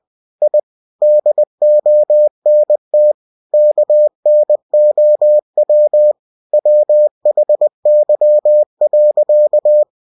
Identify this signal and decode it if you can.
Morse code, heard as a keyed tone, about 20 wpm, 600 Hz. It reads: I DONT KNOW WHY.
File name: cw-books / winnie-the-pooh / winnie-20wpm-page-0046